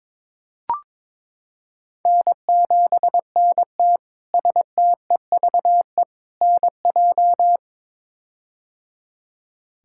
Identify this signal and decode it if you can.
Morse code, heard as a keyed tone, about 21 wpm, 700 Hz. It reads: N7NT STEVE NJ
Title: K1USN SST QSO Practice - 22wpm